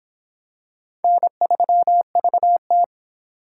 Morse code, heard as a keyed tone, about 23 wpm, 700 Hz. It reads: N3VT